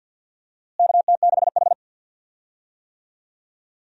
Morse code, heard as a keyed tone, about 47 wpm, 700 Hz. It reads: KT6H
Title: USA Call Signs - 4 Characters - Repeated 2x - 50wpm